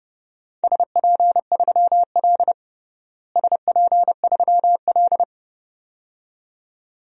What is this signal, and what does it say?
Morse code, heard as a keyed tone, about 30 wpm, 700 Hz. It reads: SP3L SP3L
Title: DXCC-100 Call Signs - 4 Characters - Repeated 2x - 30wpm